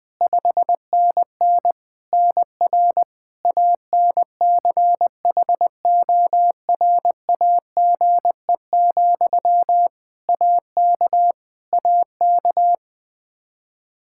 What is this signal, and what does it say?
Morse code, heard as a keyed tone, about 21 wpm, 700 Hz. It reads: 5NN NR ANCHORAGE, AK AK
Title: Practice QSO - Exchange 1 - Beginning - 20wpm